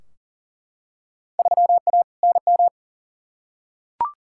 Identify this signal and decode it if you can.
Morse code, heard as a keyed tone, about 40 wpm, 700 Hz. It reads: 3A NM
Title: Field Day Exchanges Encore - 40wpm